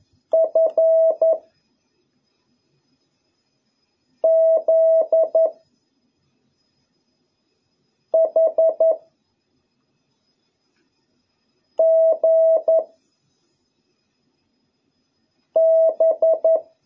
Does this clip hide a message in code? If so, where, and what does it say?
Yes: Morse code from 0.3 to 16.6 s, heard as a keyed tone, about 11 wpm, 650 Hz. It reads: F Z H G B